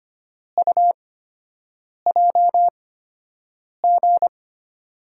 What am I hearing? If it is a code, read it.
Morse code, heard as a keyed tone, about 25 wpm, spaced as for 7 wpm, 700 Hz. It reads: UJG
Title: USA Call Signs - 6 Characters - 8x Character Spacing - 25wpm